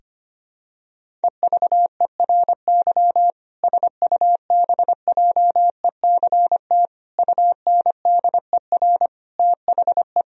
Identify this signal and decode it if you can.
Morse code, heard as a keyed tone, about 25 wpm, 700 Hz. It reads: EVERY SUBJECT UNDER THE
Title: Sentences from Top 1000 Words - Review Part 1 - 25wpm